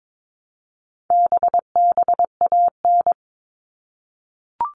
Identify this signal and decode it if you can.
Morse code, heard as a keyed tone, about 22 wpm, 700 Hz. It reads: BBAN
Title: All Cut Numbers - 4-Digit 22wpm